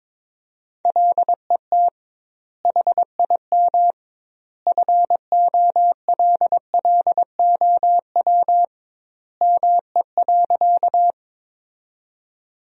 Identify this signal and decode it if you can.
Morse code, heard as a keyed tone, about 21 wpm, 700 Hz. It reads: LET HIM FOLLOW ME.
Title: 2x Word-Spacing - Sentences from Top 300 Words - 22wpm